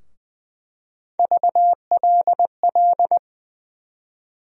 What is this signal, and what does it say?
Morse code, heard as a keyed tone, about 20 wpm, 700 Hz. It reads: VLL
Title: USA Call Sign Postfixes - 3 Letters 20wpm